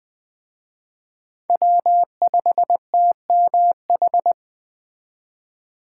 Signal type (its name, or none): Morse code